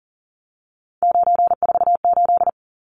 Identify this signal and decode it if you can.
Morse code, heard as a keyed tone, about 40 wpm, 700 Hz. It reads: 948
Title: Numbers - 3-Digit 40wpm